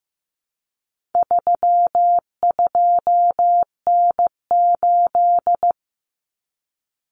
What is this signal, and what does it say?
Morse code, heard as a keyed tone, about 15 wpm, 700 Hz. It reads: 32N8
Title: Common Cut Numbers - 4-Digit 15wpm